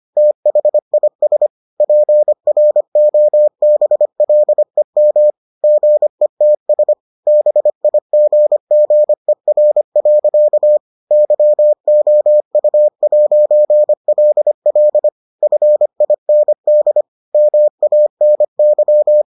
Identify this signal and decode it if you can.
Morse code, heard as a keyed tone, about 25 wpm, 600 Hz. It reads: THIS PROBLEM GETS BIGGER. YOU'LL FIND MANY